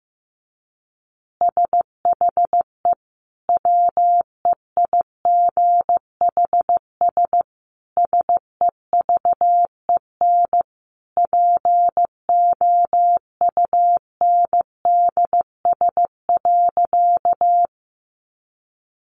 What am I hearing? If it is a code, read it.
Morse code, heard as a keyed tone, about 15 wpm, 700 Hz. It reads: SHE WEIGHS SEVEN POUNDS.
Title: Easy Phrases 15wpm